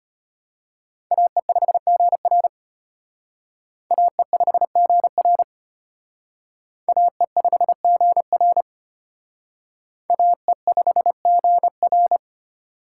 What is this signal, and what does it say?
Morse code, heard as a keyed tone, about 31 wpm, 700 Hz. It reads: AE5GR AE5GR AE5GR AE5GR
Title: Speed Racing - USA 2x2 Call Signs 25wpm